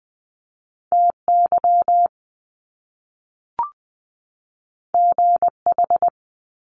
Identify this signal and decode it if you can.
Morse code, heard as a keyed tone, about 20 wpm, 700 Hz. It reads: TY GH
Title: Top Two Letter Combinations in Words 20wpm